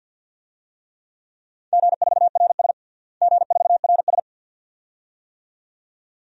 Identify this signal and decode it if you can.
Morse code, heard as a keyed tone, about 52 wpm, 700 Hz. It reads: G4RS G4RS
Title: DXCC-100 Call Signs - 4 Characters - Repeated 2x - 50wpm